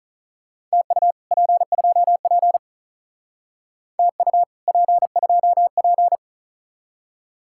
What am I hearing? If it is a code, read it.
Morse code, heard as a keyed tone, about 37 wpm, 700 Hz. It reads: TU P2P TU P2P
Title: Speed-Racing Common QSO Phrases Encore - 35wpm